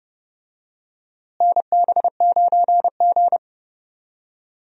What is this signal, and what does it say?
Morse code, heard as a keyed tone, about 31 wpm, 700 Hz. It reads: NB9G
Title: USA Call Signs - 4 Characters - Repeated 3x - 30wpm